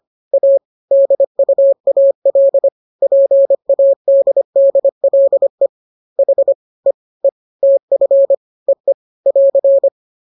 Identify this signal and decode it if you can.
Morse code, heard as a keyed tone, about 25 wpm, 550 Hz. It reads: A DUAL PADDLE H E E TF EE <AR>
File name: jokes / joke 1 25 0000